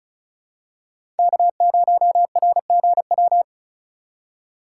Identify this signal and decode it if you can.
Morse code, heard as a keyed tone, about 36 wpm, 700 Hz. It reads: K0RGW
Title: USA Call Signs - 3x Character Spacing - 35wpm